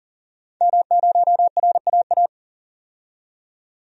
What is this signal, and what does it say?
Morse code, heard as a keyed tone, about 40 wpm, 700 Hz. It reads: M0RAA
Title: DXCC-100 Call Signs - 5 Characters - 40wpm